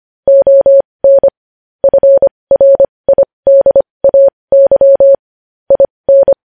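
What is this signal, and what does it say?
Morse code, heard as a keyed tone, about 25 wpm, spaced as for 20 wpm, 550 Hz. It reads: ON FRIDAY IN
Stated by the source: Morse code news headlines sent at 20 wpm.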